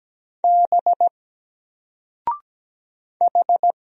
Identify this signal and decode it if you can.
Morse code, heard as a keyed tone, about 17 wpm, 700 Hz. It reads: B H
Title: Single Letter-Number 17wpm